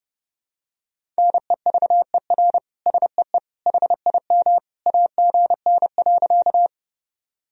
Morse code, heard as a keyed tone, about 30 wpm, 700 Hz. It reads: NEVER SEE HIM AGN.
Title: Sentences from Top 300 Words and Abbreviations 30wpm